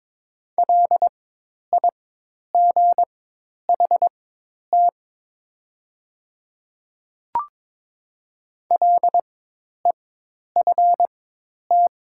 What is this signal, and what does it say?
Morse code, heard as a keyed tone, about 22 wpm, spaced as for 10 wpm, 700 Hz. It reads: LIGHT LEFT
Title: 4x Character Spacing - Top 300 Words - 22wpm